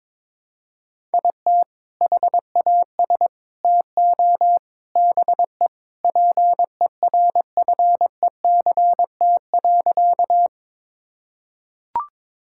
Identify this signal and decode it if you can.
Morse code, heard as a keyed tone, about 22 wpm, 700 Hz. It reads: IT HAS TO BE PERFECT.
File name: practice / 22wpm / Sentences with 3-Word Phrases - 22wpm